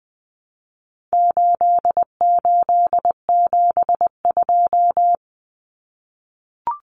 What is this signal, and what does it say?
Morse code, heard as a keyed tone, about 20 wpm, 700 Hz. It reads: 8872
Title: Numbers - 4-Digit 20wpm